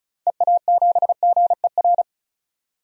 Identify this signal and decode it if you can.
Morse code, heard as a keyed tone, about 34 wpm, 700 Hz. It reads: EA7GER
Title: DXCC-100 Call Signs - 6 Characters - 35wpm